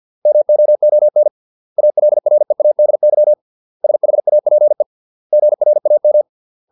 Morse code, heard as a keyed tone, about 50 wpm, 600 Hz. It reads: MOON ALREADY SHAPE GRAM